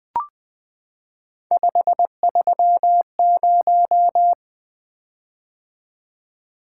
Morse code, heard as a keyed tone, about 20 wpm, 700 Hz. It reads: 530
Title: Three Letter-Number Combinations - 20wpm